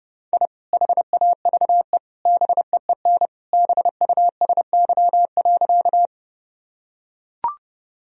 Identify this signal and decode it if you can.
Morse code, heard as a keyed tone, about 30 wpm, 700 Hz. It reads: I HAVE BEEN BUSY.
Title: Sentences from Top 1000 Words - Review Part 3 - 30wpm